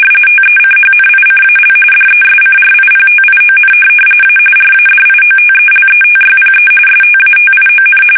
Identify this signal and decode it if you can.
RTTY, about 45.45 baud. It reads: TECHNICAL DESCRIPTION A RADIOTELETYPE STATION
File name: fsk850 / teletype-RTTY